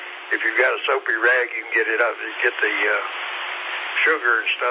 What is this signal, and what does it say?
Emergency Alert System (SAME) tones, heard as harsh buzzing bursts, with speech. It is heard from 1.1 to 4.5 s.